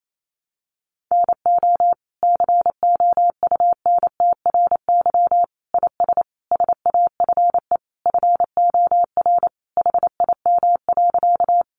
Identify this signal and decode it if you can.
Morse code, heard as a keyed tone, about 28 wpm, 700 Hz. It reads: NO COUNTRY IS SAFE FOR HIM.
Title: Sentences from Top 900 Words 28wpm